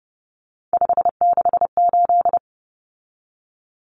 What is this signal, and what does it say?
Morse code, heard as a keyed tone, about 30 wpm, 700 Hz. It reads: 568
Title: Numbers - 3-Digit 30wpm